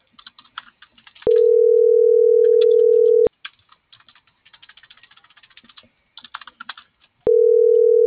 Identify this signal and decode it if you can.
Ringback tone (North America), 440 + 480 Hz.